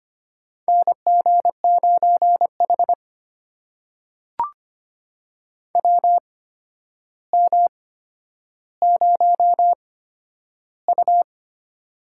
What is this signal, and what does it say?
Morse code, heard as a keyed tone, about 25 wpm, spaced as for 11 wpm, 700 Hz. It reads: NG9H WM0U
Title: USA Call Signs - 4 Characters - 8x Character Spacing - 25wpm